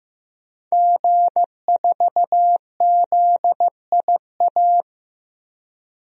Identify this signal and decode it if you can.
Morse code, heard as a keyed tone, about 15 wpm, 700 Hz. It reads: G4ZIA
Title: DXCC-100 Call Signs - 5 Characters - Repeated 3x - 15wpm